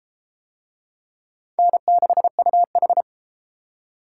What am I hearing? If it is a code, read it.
Morse code, heard as a keyed tone, about 28 wpm, 700 Hz. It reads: N6UH